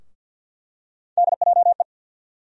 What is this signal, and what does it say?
Morse code, heard as a keyed tone, about 46 wpm, 690 Hz. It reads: DJE
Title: USA Call Sign Postfixes - 3 Letters 50wpm